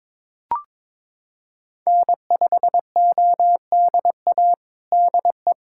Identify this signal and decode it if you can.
Morse code, heard as a keyed tone, about 22 wpm, 700 Hz. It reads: N5ODA DE
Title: USA Callsign DE Callsign - 5 Characters - 22wpm